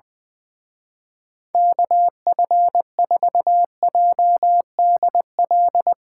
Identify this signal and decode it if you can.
Morse code, heard as a keyed tone, about 20 wpm, 700 Hz. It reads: KF4JDL
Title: Speed Racing - USA 2x3 Call Signs 20wpm